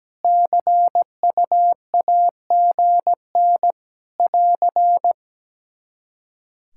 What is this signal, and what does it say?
Morse code, heard as a keyed tone, about 17 wpm, 700 Hz. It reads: CUAGN <AR>
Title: Practice QSO - Exchange 4 - Ending - 17wpm